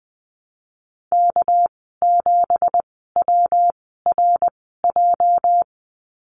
Morse code, heard as a keyed tone, about 20 wpm, 700 Hz. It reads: K 7 W R J
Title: 207 Lesson 33g - Letter j - Call Signs 20wpm